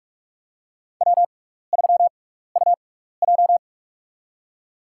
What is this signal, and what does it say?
Morse code, heard as a keyed tone, about 46 wpm, spaced as for 15 wpm, 700 Hz. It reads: W3UJ
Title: USA Call Signs - 4 Characters - 6x Character Spacing - 45wpm